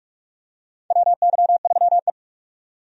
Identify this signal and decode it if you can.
Morse code, heard as a keyed tone, about 46 wpm, 700 Hz. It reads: WY3E
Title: USA Call Signs - 4 Characters - 45wpm